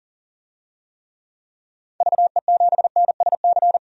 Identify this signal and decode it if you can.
Morse code, heard as a keyed tone, about 40 wpm, 700 Hz. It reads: VE7NSC
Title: DXCC-100 Call Signs - 6 Characters - 40wpm